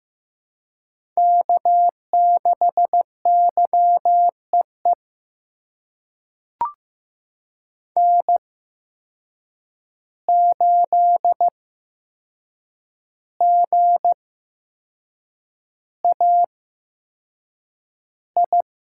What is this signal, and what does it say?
Morse code, heard as a keyed tone, about 15 wpm, spaced as for 6 wpm, 700 Hz. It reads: K6YEE N8GAI
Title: USA Call Signs - 5 Characters - 8x Character Spacing - 15wpm